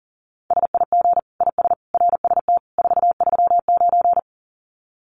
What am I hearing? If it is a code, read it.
Morse code, heard as a keyed tone, about 40 wpm, 700 Hz. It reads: SIG IS RST 439
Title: Common QSO Phrases 40wpm